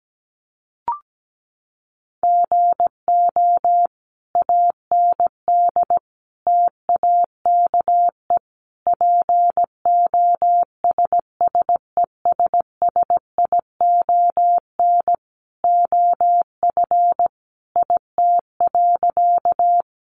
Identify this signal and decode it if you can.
Morse code, heard as a keyed tone, about 17 wpm, 700 Hz. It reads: GO AND TAKE POSSESSION OF IT.
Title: Sentences from Top 1400 Words 17wpm